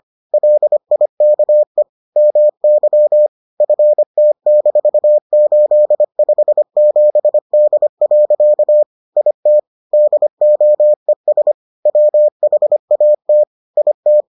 Morse code, heard as a keyed tone, about 25 wpm, 600 Hz. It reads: LIKE MY FT-857D. IT DOES WHAT IT